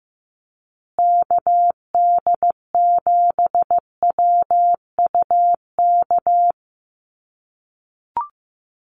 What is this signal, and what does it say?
Morse code, heard as a keyed tone, about 15 wpm, 700 Hz. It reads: KD7WUK
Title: USA Call Signs Repeated 3x 15wpm